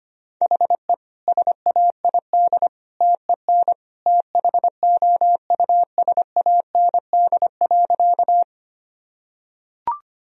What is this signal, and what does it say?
Morse code, heard as a keyed tone, about 25 wpm, 700 Hz. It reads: HE SAID TEN THOUSAND.
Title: Sentences from Top 1000 Words - Review Part 3 - 25wpm